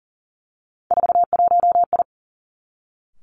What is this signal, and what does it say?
Morse code, heard as a keyed tone, about 41 wpm, 700 Hz. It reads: 41I
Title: Three Challenging Letter-Number Combinations 40wpm